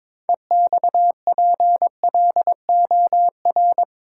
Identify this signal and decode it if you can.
Morse code, heard as a keyed tone, about 22 wpm, 700 Hz. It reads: EXPLOR
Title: Sentences from Top 1100 Words - New Word Drill - 22wpm